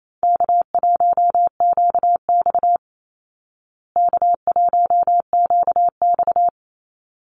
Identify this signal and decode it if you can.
Morse code, heard as a keyed tone, about 28 wpm, 700 Hz. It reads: K1QX K1QX
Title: Easy Call Signs Repeated 2x - 28wpm